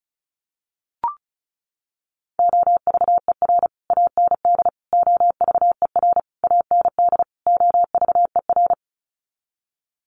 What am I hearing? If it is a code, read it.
Morse code, heard as a keyed tone, about 34 wpm, 700 Hz. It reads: OVER AND OVER AND OVER
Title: Sets of 5 Words Encore 35wpm